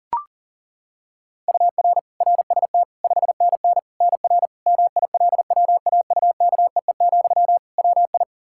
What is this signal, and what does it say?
Morse code, heard as a keyed tone, about 40 wpm, 700 Hz. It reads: UR RST 5NN NR MILWAUKEE, WI
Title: Practice QSO - Exchange 1 - Beginning - 40wpm